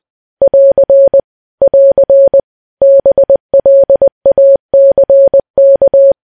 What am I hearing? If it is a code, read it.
Morse code, heard as a keyed tone, about 20 wpm, 550 Hz. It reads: <AR> <AR> BLACK
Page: news-headlines-in-morse-code-at-20-wpm